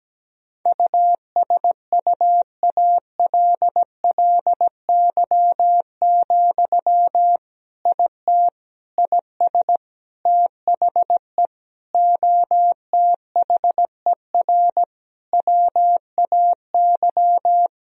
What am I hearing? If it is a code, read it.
Morse code, heard as a keyed tone, about 17 wpm, 700 Hz. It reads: USUALLY, IT IS THE OTHER WAY
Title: Sentences from Top 1000 Words - Review Part 3 - 17wpm